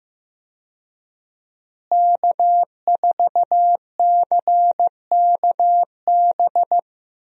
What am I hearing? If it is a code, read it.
Morse code, heard as a keyed tone, about 15 wpm, 700 Hz. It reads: K4CKB
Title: USA Call Signs - 5 Characters - Repeated 3x - 15wpm